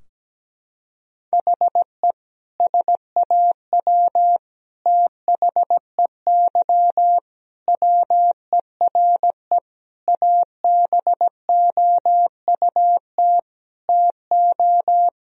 Morse code, heard as a keyed tone, about 17 wpm, 700 Hz. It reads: HE SAW THEY WERE ABOUT TO
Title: Sentences from Top 1000 Words - Review Part 4 - 17wpm